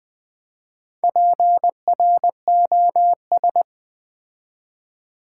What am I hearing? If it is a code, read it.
Morse code, heard as a keyed tone, about 20 wpm, 700 Hz. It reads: PROS
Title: Build A Word - 4 Letters At A Time - 10-Letter Words - 20wpm